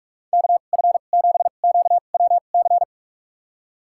Morse code, heard as a keyed tone, about 43 wpm, spaced as for 32 wpm, 700 Hz. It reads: KF7QWC
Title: USA Call Signs - 6 Characters - 2x Character Spacing - 45wpm